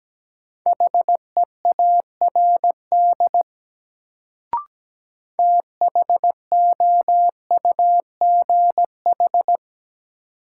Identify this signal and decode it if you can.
Morse code, heard as a keyed tone, about 17 wpm, 700 Hz. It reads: HEARD THOUGH
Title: Sentences from Top 400 Words - New Word Drill - 17wpm